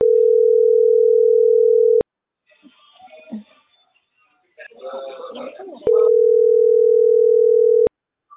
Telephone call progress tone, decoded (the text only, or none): ringback tone